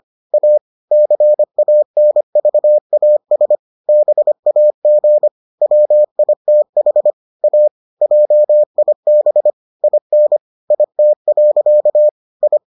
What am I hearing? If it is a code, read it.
Morse code, heard as a keyed tone, about 25 wpm, 600 Hz. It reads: A CANVAS BAG WITH A JIB IN IT. I